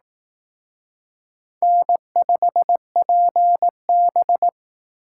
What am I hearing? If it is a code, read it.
Morse code, heard as a keyed tone, about 18 wpm, 700 Hz. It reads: N5PB
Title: Speed Racing - USA 1x2 Call Signs 15wpm